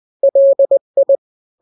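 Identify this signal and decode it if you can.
Morse code, heard as a keyed tone, about 20 wpm, 550 Hz. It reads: LI
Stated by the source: Morse drill of two-letter n-grams at 20/18 WPM (Farnsworth), from KK7RR.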